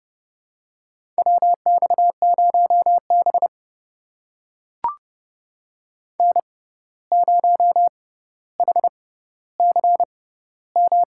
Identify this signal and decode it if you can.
Morse code, heard as a keyed tone, about 30 wpm, spaced as for 14 wpm, 700 Hz. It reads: WX0B N0HCM
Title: USA Call Signs - 6x Character Spacing - 30wpm